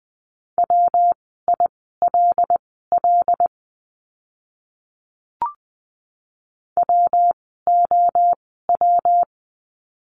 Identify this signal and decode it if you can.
Morse code, heard as a keyed tone, about 20 wpm, spaced as for 14 wpm, 700 Hz. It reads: WILL WOW
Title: 115 Lesson 19f - Letter w - Words 20wpm